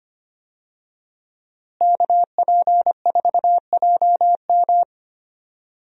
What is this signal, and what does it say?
Morse code, heard as a keyed tone, about 25 wpm, 700 Hz. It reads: KP4JM
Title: DXCC-100 Call Signs - 5 Characters - Repeated 3x - 25wpm